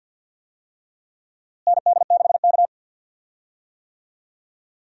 Morse code, heard as a keyed tone, about 48 wpm, 700 Hz. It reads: ND6K
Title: USA Call Signs - 4 Characters - Repeated 3x - 50wpm